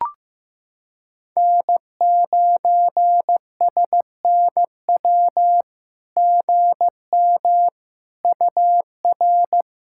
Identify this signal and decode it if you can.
Morse code, heard as a keyed tone, about 15 wpm, 700 Hz. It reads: N9SNW GM UR
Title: SOTA QSO Practice 15wpm